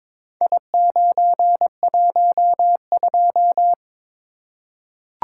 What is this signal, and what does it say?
Morse code, heard as a keyed tone, about 22 wpm, 700 Hz. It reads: I912
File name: practice / 22wpm / Four Letter-Number Combinations - 22wpm